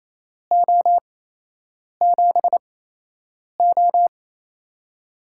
Morse code, heard as a keyed tone, about 27 wpm, spaced as for 8 wpm, 700 Hz. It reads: O7O
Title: USA Call Signs - 4 Characters - 8x Character Spacing - 28wpm